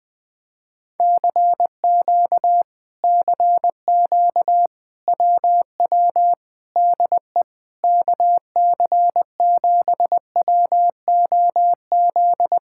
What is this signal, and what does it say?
Morse code, heard as a keyed tone, about 20 wpm, 700 Hz. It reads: CQ CQ WW DE KC7WOZ
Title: Speed-Racing Common QSO Phrases - 17wpm